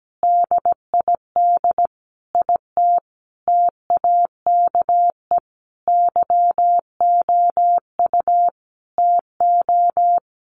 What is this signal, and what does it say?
Morse code, heard as a keyed tone, about 17 wpm, 700 Hz. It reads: DID IT TAKE YOU TO
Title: Sentences from Top 1500 Words 17wpm